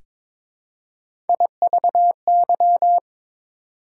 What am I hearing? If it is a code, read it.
Morse code, heard as a keyed tone, about 22 wpm, 690 Hz. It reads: IVY
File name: Difficult Words with Hints - 22wpm